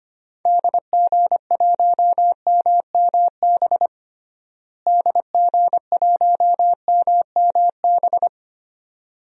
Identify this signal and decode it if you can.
Morse code, heard as a keyed tone, about 25 wpm, 700 Hz. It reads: DG1MMB DG1MMB
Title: DXCC-100 Call Signs - 6 Characters - Repeated 2x - 25wpm